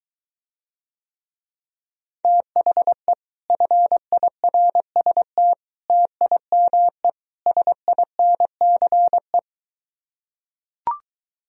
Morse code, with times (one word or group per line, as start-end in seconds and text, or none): 2.2-3.1 THE
3.5-5.5 FIRST
5.9-7.1 TIME
7.5-9.4 SINCE